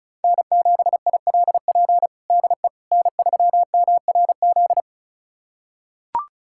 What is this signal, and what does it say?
Morse code, heard as a keyed tone, about 34 wpm, 700 Hz. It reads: N7ILP DE N3MRZ
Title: USA Callsign DE Callsign - 5 Characters - 35wpm